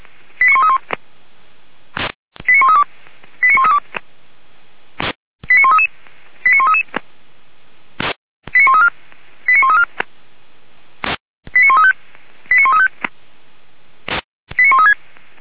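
Five-tone selective call (ZVEI), heard as quick runs of five tones, tones of about 70 ms. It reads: 89131 89132 89132 89133 89133 89134 89134 89135 89135 89136
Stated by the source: ZVEI five-tone selective call received by SDR.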